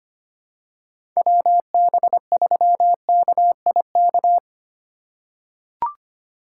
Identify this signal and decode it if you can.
Morse code, heard as a keyed tone, about 25 wpm, 700 Hz. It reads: WB3KIK
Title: USA Call Signs - 4x Character Spacing - 25wpm